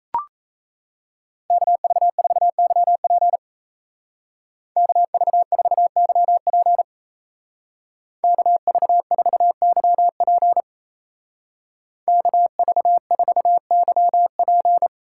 Morse code, heard as a keyed tone, about 35 wpm, 700 Hz. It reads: KV4YP KV4YP KV4YP KV4YP
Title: Speed Racing - USA Call Signs 28wpm